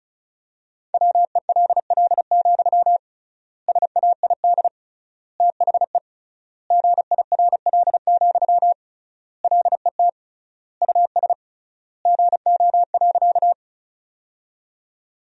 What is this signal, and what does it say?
Morse code, heard as a keyed tone, about 36 wpm, 700 Hz. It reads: WELL, SAID THE GIRL, LET US GO.
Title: 3x Word-Spacing - Sentences from Top 400 Words - 35wpm